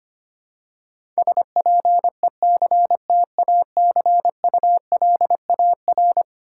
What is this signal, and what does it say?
Morse code, heard as a keyed tone, about 25 wpm, 700 Hz. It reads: SPECTACULAR
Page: morse-code-ninja-practice-25wpm